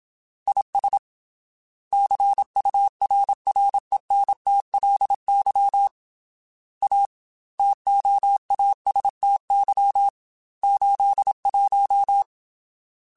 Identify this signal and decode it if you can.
Morse code, heard as a keyed tone, about 27 wpm, 800 Hz. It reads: IS CURRENTLY A TOASTY 81
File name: Weather Sets / Weather Sets-mp3s / Weather 5 12 @25